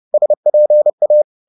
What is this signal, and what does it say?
Morse code, heard as a keyed tone, about 30 wpm, 600 Hz. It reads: SPA
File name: Korte ord DK 30wpm